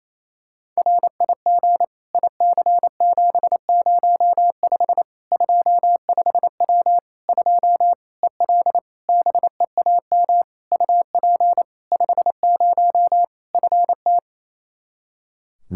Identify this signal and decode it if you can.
Morse code, heard as a keyed tone, about 28 wpm, 700 Hz. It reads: RIG IC705 25W 2 EL BEAM UP 50 FT
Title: Practice QSO - Exchange 2 - Info - 28wpm